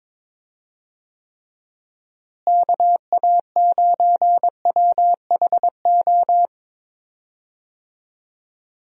Morse code, heard as a keyed tone, about 21 wpm, 700 Hz. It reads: KA9WHO
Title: USA Call Signs - 6 Characters - Repeated 2x - 22wpm